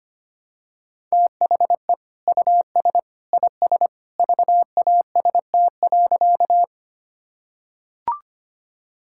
Morse code, heard as a keyed tone, about 25 wpm, 700 Hz. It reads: THE US IS VAST.
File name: practice / 25wpm / Sentences with 2-Word Phrases - 25wpm